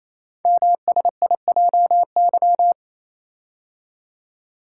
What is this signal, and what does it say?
Morse code, heard as a keyed tone, about 27 wpm, 700 Hz. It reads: MSIJY
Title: Code Groups - 5 Letters - 28wpm